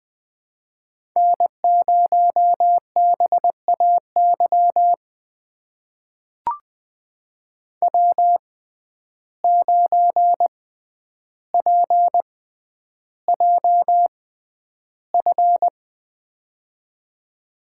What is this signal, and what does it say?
Morse code, heard as a keyed tone, about 20 wpm, spaced as for 10 wpm, 700 Hz. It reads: N0BAY W9PJF
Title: USA Call Signs - 5 Characters - 6x Character Spacing - 20wpm